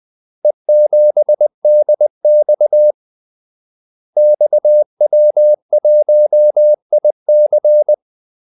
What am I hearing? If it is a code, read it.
Morse code, heard as a keyed tone, about 20 wpm, 600 Hz. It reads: E7DX XW1IC
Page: Kaldesignaler 20 wpm